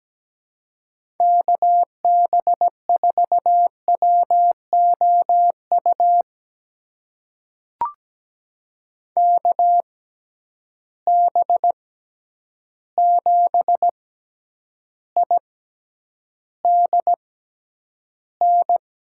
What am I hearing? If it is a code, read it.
Morse code, heard as a keyed tone, about 17 wpm, spaced as for 9 wpm, 700 Hz. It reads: KB4WOU KB7IDN
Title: USA Call Signs - 6 Characters - 6x Character Spacing - 17wpm